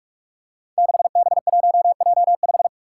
Morse code, heard as a keyed tone, about 44 wpm, 700 Hz. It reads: 6B1J5
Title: Five Challenging Letter-Number Combinations 45wpm